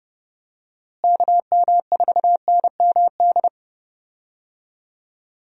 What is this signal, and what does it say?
Morse code, heard as a keyed tone, about 32 wpm, 700 Hz. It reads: KM4NMD
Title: USA Call Signs - 6 Characters - Repeated 2x - 30wpm